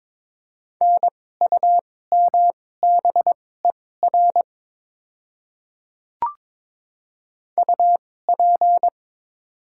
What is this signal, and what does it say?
Morse code, heard as a keyed tone, about 22 wpm, spaced as for 16 wpm, 700 Hz. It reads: NUMBER UP
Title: 2x Character Spacing - Top 100 Words - 22wpm